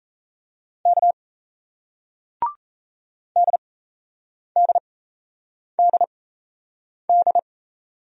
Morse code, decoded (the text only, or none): K DDDD